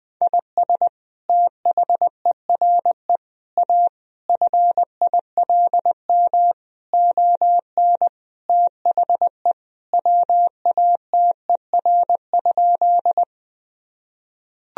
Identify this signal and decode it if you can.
Morse code, heard as a keyed tone, about 21 wpm, 700 Hz. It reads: IS THERE A FILM ON THE WATER?
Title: Sentences from Top 1000 Words - Review Part 3 - 20wpm